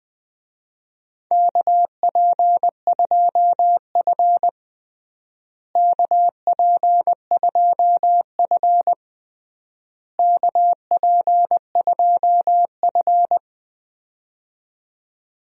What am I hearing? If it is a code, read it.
Morse code, heard as a keyed tone, about 21 wpm, 700 Hz. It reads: KP2F KP2F KP2F
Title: DXCC-100 Call Signs - 4 Characters - Repeated 3x - 20wpm